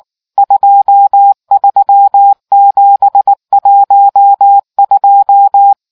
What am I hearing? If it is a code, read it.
Morse code, heard as a keyed tone, about 19 wpm, 800 Hz. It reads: 23712